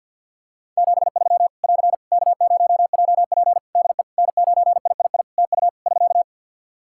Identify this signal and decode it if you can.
Morse code, heard as a keyed tone, about 50 wpm, 700 Hz. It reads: 73 <AR> K0JP DE N9III TU <SK>
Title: Practice QSO - Exchange 4 - Ending - 50wpm